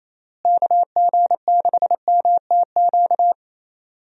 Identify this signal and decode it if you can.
Morse code, heard as a keyed tone, about 28 wpm, 700 Hz. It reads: KG6MTQ
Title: USA Call Signs - 6 Characters - Repeated 3x - 28wpm